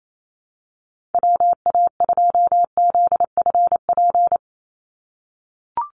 Morse code, heard as a keyed tone, about 28 wpm, 700 Hz. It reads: WA2ZFP
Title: USA Call Signs 28wpm